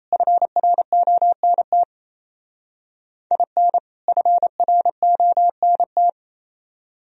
Morse code, heard as a keyed tone, about 34 wpm, 700 Hz. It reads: FRONT IN FRONT
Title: Speed-Racing - Sets of 2 Words 28wpm